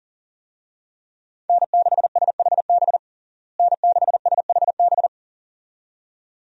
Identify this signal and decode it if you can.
Morse code, heard as a keyed tone, about 40 wpm, 700 Hz. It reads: N6SHB N6SHB
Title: USA Call Signs - 5 Characters - Repeated 2x - 40wpm